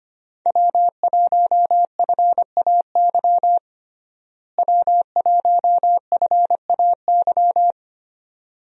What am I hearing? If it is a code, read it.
Morse code, heard as a keyed tone, about 25 wpm, 700 Hz. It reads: W1FAY W1FAY
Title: USA Call Signs - 5 Characters - Repeated 2x - 25wpm